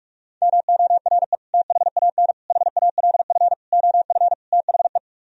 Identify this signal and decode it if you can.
Morse code, heard as a keyed tone, about 49 wpm, 700 Hz. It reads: MORE THAN HALF OF THE